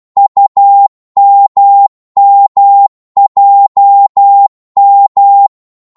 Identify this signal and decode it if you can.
Morse code, heard as a keyed tone, about 12 wpm, 800 Hz. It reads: UMMJM
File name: Morse / ltr-12wpm